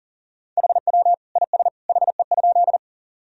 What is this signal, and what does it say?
Morse code, heard as a keyed tone, about 40 wpm, 700 Hz. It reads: HW IS HE?
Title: Sentences from Top 100 Words and Abbreviations 40wpm